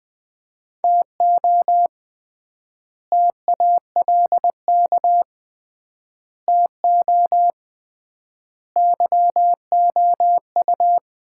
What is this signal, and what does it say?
Morse code, heard as a keyed tone, about 21 wpm, 700 Hz. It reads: TO TALK TO YOU
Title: 3x Word-Spacing - Sets of 4 Words - 20wpm